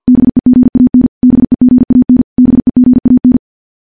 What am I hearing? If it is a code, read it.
Morse code, heard as a keyed tone, about 50 wpm, 250 Hz. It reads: BEGAN BEGAN BEGAN
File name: morse / morse-training-50WPM